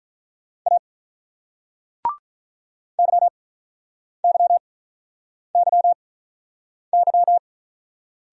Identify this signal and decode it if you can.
Morse code, heard as a keyed tone, about 46 wpm, spaced as for 9 wpm, 700 Hz. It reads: A YYYY